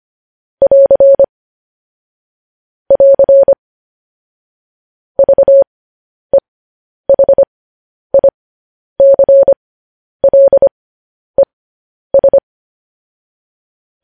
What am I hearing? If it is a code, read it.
Morse code, heard as a keyed tone, about 25 wpm, spaced as for 10 wpm, 550 Hz. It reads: <AR> <AR> VEHICLES
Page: news headlines in morse code at 10 wpm